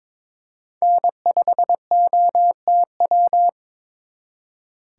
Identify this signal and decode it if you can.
Morse code, heard as a keyed tone, about 26 wpm, 700 Hz. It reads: N5OTW
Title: Speed Racing - USA 1x3 Call Signs 22wpm